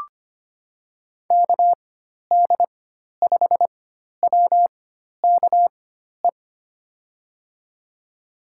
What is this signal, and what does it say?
Morse code, heard as a keyed tone, about 25 wpm, spaced as for 12 wpm, 700 Hz. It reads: KD5WKE
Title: USA Call Signs - 6 Characters - 4x Character Spacing - 25wpm